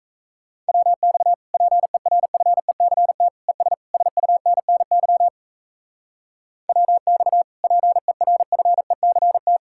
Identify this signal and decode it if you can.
Morse code, heard as a keyed tone, about 34 wpm, 700 Hz. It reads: WX PERFECT ES SUNNY WX PERFECT
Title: Speed-Racing Common QSO Phrases Encore - 28wpm